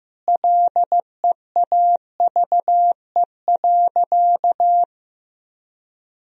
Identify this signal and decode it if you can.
Morse code, heard as a keyed tone, about 15 wpm, 700 Hz. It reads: LEAVE.
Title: Sentences from Top 1000 Words - Review Part 4 - 15wpm